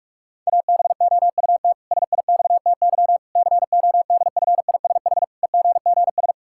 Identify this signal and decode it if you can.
Morse code, heard as a keyed tone, about 41 wpm, 700 Hz. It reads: ABOUT SIXTY CODFISH EGGS
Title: Pangram Challenge v2 - 45wpm